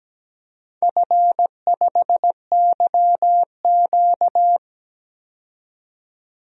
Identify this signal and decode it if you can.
Morse code, heard as a keyed tone, about 17 wpm, 700 Hz. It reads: F5YQ
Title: DXCC-100 Call Signs - 4 Characters - 17wpm